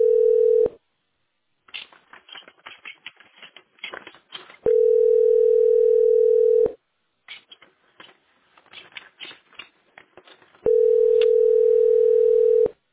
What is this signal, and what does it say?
Ringback tone (North America), 440 + 480 Hz.
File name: f98214b8-51ca-4fc1-966c